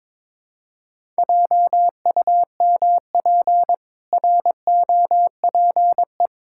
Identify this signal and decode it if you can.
Morse code, heard as a keyed tone, about 22 wpm, 700 Hz. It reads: JUMP ROPE
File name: Difficult Words with Hints - 22wpm